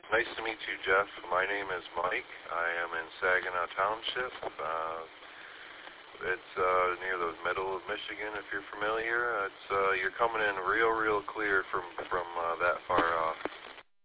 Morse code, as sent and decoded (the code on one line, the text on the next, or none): none